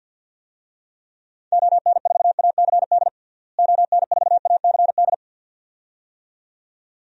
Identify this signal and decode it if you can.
Morse code, heard as a keyed tone, about 50 wpm, 700 Hz. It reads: ON4ACD ON4ACD
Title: DXCC-100 Call Signs - 6 Characters - Repeated 2x - 50wpm